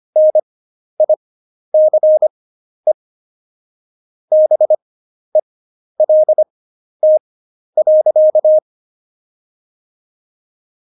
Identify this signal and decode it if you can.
Morse code, heard as a keyed tone, about 25 wpm, spaced as for 11 wpm, 620 Hz. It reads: NICE BELT.
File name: Jokes-25-char-10-eff